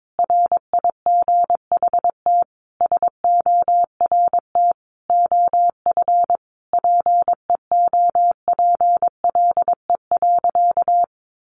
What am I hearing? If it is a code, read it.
Morse code, heard as a keyed tone, about 22 wpm, 700 Hz. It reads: RIGHT SORT OF PEOPLE.
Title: Sentences from Top 1300 Words 22wpm